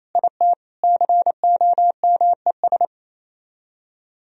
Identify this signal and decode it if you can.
Morse code, heard as a keyed tone, about 36 wpm, 700 Hz. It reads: IT COMES